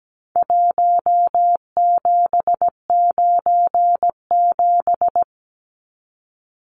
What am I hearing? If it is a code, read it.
Morse code, heard as a keyed tone, about 17 wpm, 700 Hz. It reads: 1797
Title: Numbers - Contest Style - 4-Digit 17wpm